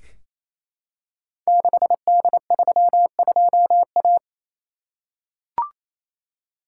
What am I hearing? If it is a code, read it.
Morse code, heard as a keyed tone, about 27 wpm, 700 Hz. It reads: 6D32A
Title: Code Groups - 5 Letter:Numbers - 28wpm